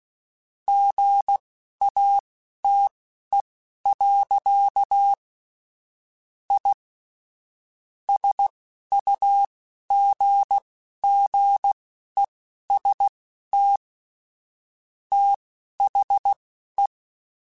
Morse code, heard as a keyed tone, about 16 wpm, spaced as for 11 wpm, 800 Hz. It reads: GATE. I SUGGEST THE
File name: Technical Subjects / Technical Subjects-mp3s / AND Gate @10